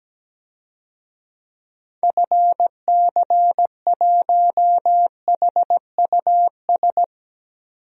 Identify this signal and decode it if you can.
Morse code, heard as a keyed tone, about 17 wpm, 700 Hz. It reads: FC1HUS
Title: DXCC-100 Call Signs - 6 Characters - 17wpm